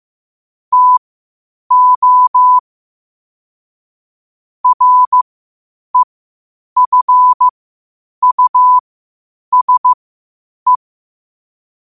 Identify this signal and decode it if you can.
Morse code, heard as a keyed tone, about 15 wpm, spaced as for 8 wpm, 1000 Hz. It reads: TO REFUSE
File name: qotdfiles / 8wpm